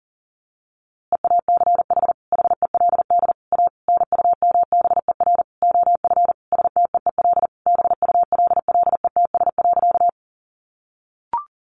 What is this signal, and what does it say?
Morse code, heard as a keyed tone, about 40 wpm, 700 Hz. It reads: EACH HELD A NUMBER OF STEEL BULLETS.
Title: A Sound Of Thunder 40wpm